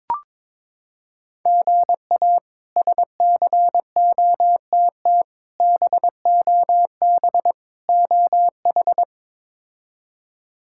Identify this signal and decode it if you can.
Morse code, heard as a keyed tone, about 22 wpm, 700 Hz. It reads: GA SCOTT BOB OH
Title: K1USN SST QSO Practice - 22wpm